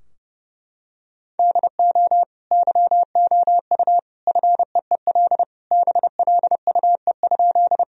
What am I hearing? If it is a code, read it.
Morse code, heard as a keyed tone, about 31 wpm, 700 Hz. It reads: DO YOU FEEL BLUE?
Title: Sentences from Top 1000 Words - Review Part 3 - 30wpm